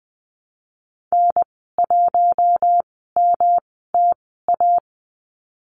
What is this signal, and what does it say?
Morse code, heard as a keyed tone, about 20 wpm, spaced as for 15 wpm, 700 Hz. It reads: N1MTA
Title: 140 Lesson 23g - Letter f - Call Signs 20wpm